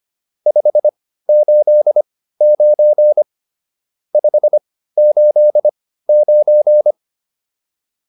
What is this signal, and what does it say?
Morse code, heard as a keyed tone, about 25 wpm, spaced as for 15 wpm, 600 Hz. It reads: 589 589
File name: QSO409 15